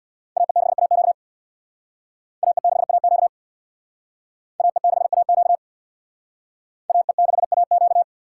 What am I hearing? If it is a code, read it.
Morse code, heard as a keyed tone, about 60 wpm, 700 Hz. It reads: AE6AQ AE6AQ AE6AQ AE6AQ
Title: Speed Racing - USA 2x2 Call Signs 50wpm